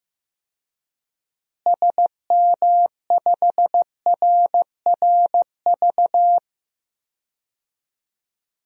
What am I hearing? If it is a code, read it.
Morse code, heard as a keyed tone, about 15 wpm, 700 Hz. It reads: SM5RRV
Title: DXCC-100 Call Signs - 6 Characters - Repeated 3x - 15wpm